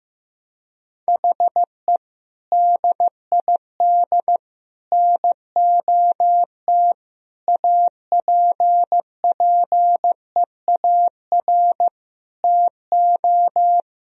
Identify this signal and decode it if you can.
Morse code, heard as a keyed tone, about 16 wpm, 700 Hz. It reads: HE DID NOT APPEAR TO
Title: Sentences from Top 1000 Words - Review Part 2 - 15wpm